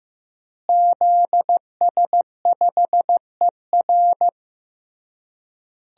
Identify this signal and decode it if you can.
Morse code, heard as a keyed tone, about 15 wpm, 700 Hz. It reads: ZS5ER
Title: DXCC-100 Call Signs - 5 Characters - Repeated 3x - 15wpm